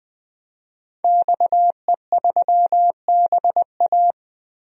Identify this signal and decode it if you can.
Morse code, heard as a keyed tone, about 20 wpm, 700 Hz. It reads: XE3BA
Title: DXCC-100 Call Signs - 5 Characters - Repeated 3x - 20wpm